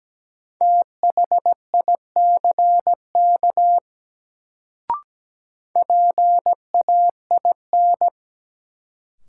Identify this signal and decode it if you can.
Morse code, heard as a keyed tone, about 17 wpm, 700 Hz. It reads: THICK PAIN
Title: Sentences from Top 1600 Words - New Word Drill Alt - 17wpm